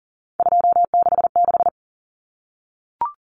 Morse code, heard as a keyed tone, about 40 wpm, 700 Hz. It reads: 266
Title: Common Cut Numbers - 3-Digit 40wpm